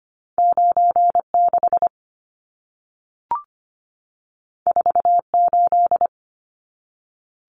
Morse code, heard as a keyed tone, about 25 wpm, 700 Hz. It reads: 96 48
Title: Numbers - 2-Digit 25wpm